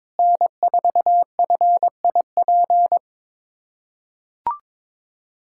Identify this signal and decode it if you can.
Morse code, heard as a keyed tone, about 22 wpm, 700 Hz. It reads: N4FIP
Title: USA Callsign DE Callsign - 5 Characters - 22wpm